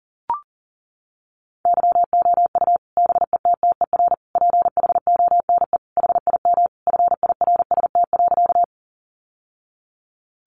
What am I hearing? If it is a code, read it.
Morse code, heard as a keyed tone, about 40 wpm, 700 Hz. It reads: YOU BETTER PHONE HIM FIRST.
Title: Sentences from Top 1100 Words 40wpm